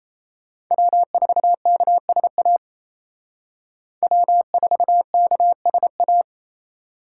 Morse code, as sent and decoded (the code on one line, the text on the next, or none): .-- ....- -.- ... .- / .-- ....- -.- ... .-
W4KSA W4KSA